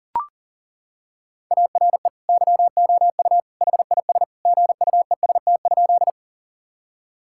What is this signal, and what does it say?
Morse code, heard as a keyed tone, about 40 wpm, 700 Hz. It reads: ARE YOU HIS GUEST?
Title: Sentences from Top 1600 Words 40wpm